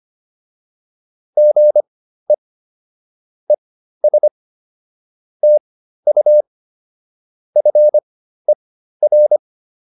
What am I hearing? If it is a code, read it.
Morse code, heard as a keyed tone, about 25 wpm, spaced as for 13 wpm, 600 Hz. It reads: GE ES TU FER
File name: QSO 108 13